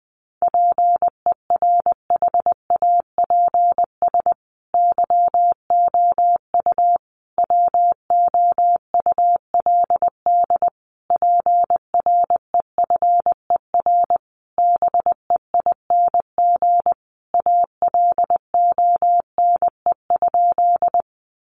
Morse code, heard as a keyed tone, about 20 wpm, 700 Hz. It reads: PERHAPS YOU WOULD PREFER BEING ALONE?
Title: Sentences from Top 1100 Words 20wpm